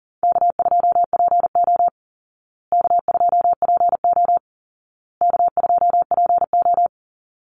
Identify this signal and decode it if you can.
Morse code, heard as a keyed tone, about 40 wpm, 700 Hz. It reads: K2PO K2PO K2PO
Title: Call Signs Repeated 2x - 40wpm